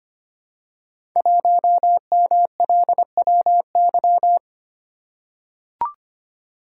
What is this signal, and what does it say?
Morse code, heard as a keyed tone, about 25 wpm, 700 Hz. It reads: 1MLWY
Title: Code Groups - 5 Letter:Numbers - 25wpm